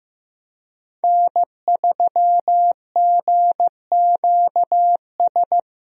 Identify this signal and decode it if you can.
Morse code, heard as a keyed tone, about 15 wpm, 700 Hz. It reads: N3GQS
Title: USA Call Signs - 5 Characters - 15wpm